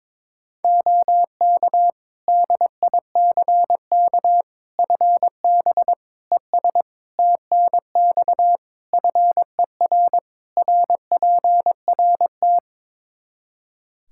Morse code, heard as a keyed tone, about 22 wpm, 700 Hz. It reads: OK DICK FB ES TNX FER RPRT
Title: Practice QSO - Exchange 2 - Info - 22wpm